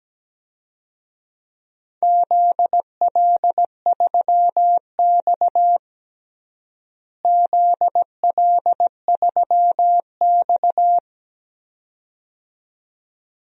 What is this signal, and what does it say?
Morse code, heard as a keyed tone, about 17 wpm, 700 Hz. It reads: ZL3X ZL3X
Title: DXCC-100 Call Signs - 4 Characters - Repeated 2x - 17wpm